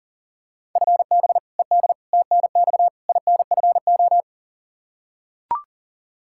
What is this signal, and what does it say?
Morse code, heard as a keyed tone, about 40 wpm, 700 Hz. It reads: FB ED TNX INFO
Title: K1USN SST QSO Practice 40wpm